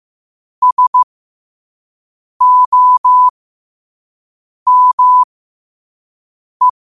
Morse code, heard as a keyed tone, about 15 wpm, spaced as for 5 wpm, 1000 Hz. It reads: SOME
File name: qotdfiles / 5wpm